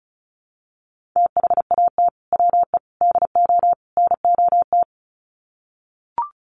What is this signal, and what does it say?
Morse code, heard as a keyed tone, about 34 wpm, 700 Hz. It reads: THAT WE DO NOT
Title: Sets of 4 Words 35wpm